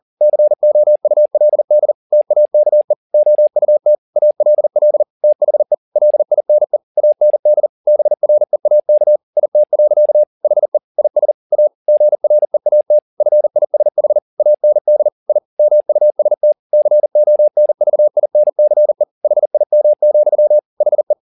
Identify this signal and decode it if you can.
Morse code, heard as a keyed tone, about 40 wpm, 600 Hz. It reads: COULD TAKE OUT ALL THE LINE AND BREAK IT. HE IS A GREAT FISH AND I MUST CONVINCE HIM, HE